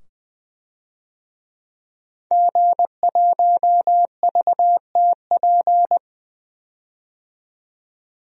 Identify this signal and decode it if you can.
Morse code, heard as a keyed tone, about 20 wpm, 700 Hz. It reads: G1VTP
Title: DXCC-100 Call Signs - 5 Characters - 20wpm